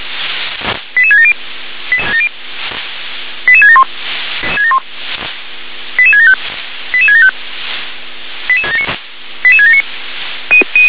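Five-tone selective call (ZVEI), heard as quick runs of five tones, tones of about 70 ms. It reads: 88670 88671 88675 88675 88679
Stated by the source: ZVEI five-tone call received by SDR.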